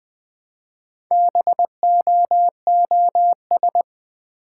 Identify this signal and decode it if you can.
Morse code, heard as a keyed tone, about 20 wpm, 700 Hz. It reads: BOOS
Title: Build A Word - 1 Letter At A Time - 5-Letter Words - 20wpm